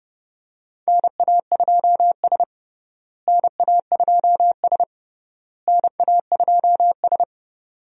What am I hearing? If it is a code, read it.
Morse code, heard as a keyed tone, about 30 wpm, 700 Hz. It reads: NA2S NA2S NA2S
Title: USA Call Signs - 4 Characters - Repeated 3x - 30wpm